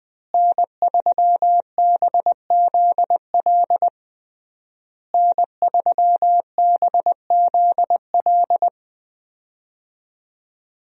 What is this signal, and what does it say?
Morse code, heard as a keyed tone, about 20 wpm, 700 Hz. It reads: N3BZL N3BZL
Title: USA Call Signs - 5 Characters - Repeated 3x - 20wpm